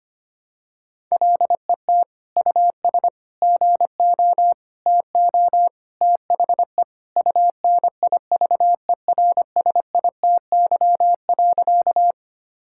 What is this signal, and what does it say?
Morse code, heard as a keyed tone, about 25 wpm, 700 Hz. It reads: LET US GO TO THE UNIVERSITY.
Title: Sentences from Top 1000 Words - Review Part 5 - 25wpm